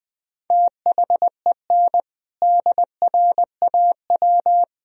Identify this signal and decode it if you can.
Morse code, heard as a keyed tone, about 21 wpm, 700 Hz. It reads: THEN DRAW
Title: Sentences from Top 1000 Words - Review Part 1 - 20wpm